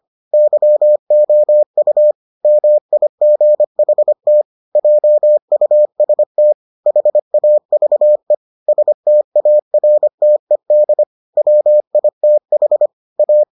Morse code, heard as a keyed tone, about 25 wpm, 600 Hz. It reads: YOU MIGHT JUST HAVE STARTED WITH A